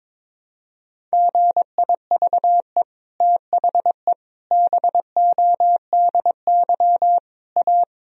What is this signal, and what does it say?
Morse code, heard as a keyed tone, about 21 wpm, 700 Hz. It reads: GIVE THE BODY A
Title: Sentences from Top 1000 Words - Review Part 2 - 22wpm